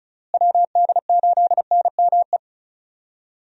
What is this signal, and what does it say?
Morse code, heard as a keyed tone, about 34 wpm, 700 Hz. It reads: WD8NME
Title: USA Call Signs - 6 Characters - Repeated 2x - 35wpm